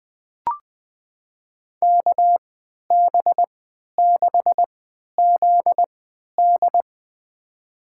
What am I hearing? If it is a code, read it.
Morse code, heard as a keyed tone, about 20 wpm, spaced as for 11 wpm, 700 Hz. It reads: KB6ZD
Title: USA Call Signs - 3x Character Spacing - 20wpm